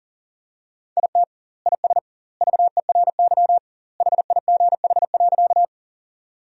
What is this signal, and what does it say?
Morse code, heard as a keyed tone, about 40 wpm, 700 Hz. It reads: IT IS VERY HIGH.
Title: 2x Word-Spacing - Sentences from Top 200 Words - 40wpm